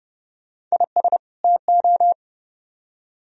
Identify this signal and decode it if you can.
Morse code, heard as a keyed tone, about 29 wpm, 700 Hz. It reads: IS TO